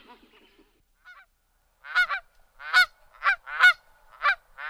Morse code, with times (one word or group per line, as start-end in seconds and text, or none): none